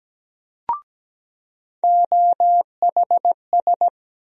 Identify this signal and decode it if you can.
Morse code, heard as a keyed tone, about 17 wpm, 700 Hz. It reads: OHS
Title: USA Call Sign Postfixes - 3 Letters 17wpm